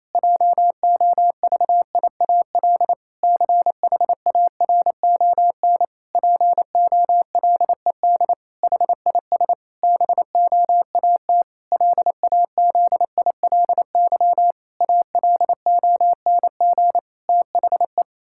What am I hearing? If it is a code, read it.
Morse code, heard as a keyed tone, about 27 wpm, 700 Hz. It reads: JOVIAL CHARON POLED HIS BOAT LAZILY ALONG THE
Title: Pangram Challenge v2 - 28wpm